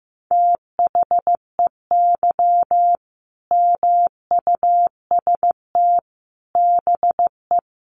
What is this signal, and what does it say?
Morse code, heard as a keyed tone, about 15 wpm, 700 Hz. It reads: THEY MUST BE
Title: Sentences from Top 1400 Words 15wpm